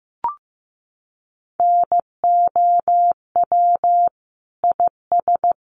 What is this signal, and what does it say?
Morse code, heard as a keyed tone, about 15 wpm, 700 Hz. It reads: NOW IS
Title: Sentences from Top 100 Words 15wpm